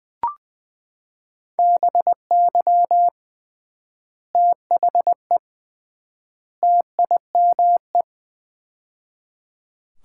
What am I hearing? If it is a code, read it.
Morse code, heard as a keyed tone, about 21 wpm, 700 Hz. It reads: BY THE TIME
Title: 3x Word-Spacing - Sets of 3 Words - 20wpm